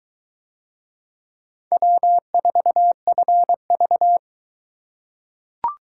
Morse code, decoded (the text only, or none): W4FV